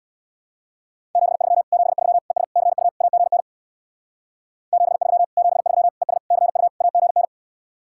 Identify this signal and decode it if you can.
Morse code, heard as a keyed tone, about 67 wpm, 700 Hz. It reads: 73 73 ES CU AGN 73 73 ES CU AGN